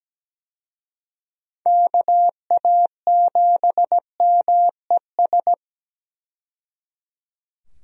Morse code, heard as a keyed tone, about 17 wpm, 700 Hz. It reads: KA7MES
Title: USA Call Signs - 6 Characters - 17wpm